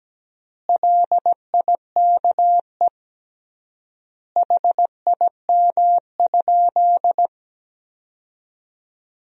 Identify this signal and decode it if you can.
Morse code, heard as a keyed tone, about 17 wpm, 700 Hz. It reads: LIKE HIM?
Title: 3x Word-Spacing - Sentences from Top 100 Words - 17wpm